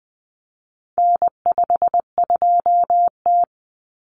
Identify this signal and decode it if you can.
Morse code, heard as a keyed tone, about 20 wpm, 700 Hz. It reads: N52T
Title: Common Cut Numbers - 4-Digit 20wpm